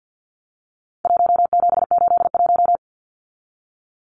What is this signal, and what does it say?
Morse code, heard as a keyed tone, about 49 wpm, 700 Hz. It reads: 1781
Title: Numbers - Contest Style - 4-Digit 50wpm